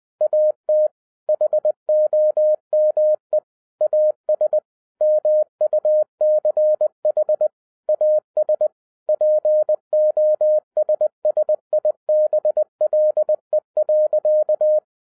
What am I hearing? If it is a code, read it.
Morse code, heard as a keyed tone, about 20 wpm, 610 Hz. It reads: AT HOME AS MUCH AS POSSIBLE.
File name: 2011-07-mp3 / n58 BACK FROM A BIG ACCIDENT 20wpm